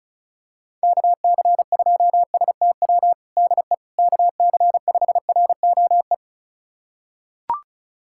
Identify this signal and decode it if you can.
Morse code, heard as a keyed tone, about 34 wpm, 700 Hz. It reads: KC2STW DE KC5ROE
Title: USA Callsign DE Callsign - 6 Characters - 35wpm